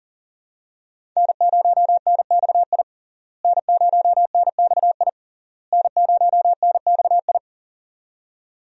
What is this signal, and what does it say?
Morse code, heard as a keyed tone, about 37 wpm, 700 Hz. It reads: N0NXI N0NXI N0NXI
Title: USA Call Signs - 5 Characters - Repeated 3x - 40wpm